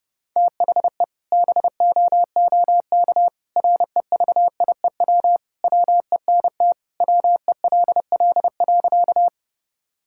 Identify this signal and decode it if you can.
Morse code, heard as a keyed tone, about 30 wpm, 700 Hz. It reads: THE BOOK REVIEW WENT WELL.
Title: Sentences from Top 1000 Words - Review Part 6 - 30wpm